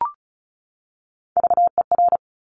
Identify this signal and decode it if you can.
Morse code, heard as a keyed tone, about 36 wpm, 700 Hz. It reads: VER
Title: Top Three Letter Combinations in Words 35wpm